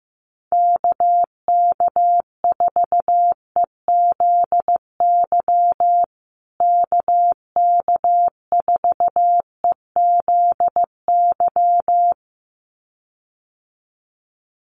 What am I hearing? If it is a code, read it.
Morse code, heard as a keyed tone, about 15 wpm, 700 Hz. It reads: KK4EZY KK4EZY
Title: USA Call Signs Repeated 2x 15wpm